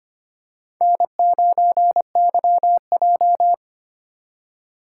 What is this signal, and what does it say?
Morse code, heard as a keyed tone, about 25 wpm, 700 Hz. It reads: N9YJ
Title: USA Call Signs - 4 Characters - 3x Character Spacing - 25wpm